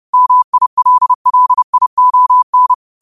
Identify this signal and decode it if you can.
Morse code, heard as a keyed tone, about 30 wpm, 1000 Hz. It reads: MILLION
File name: qotdfiles / 30wpm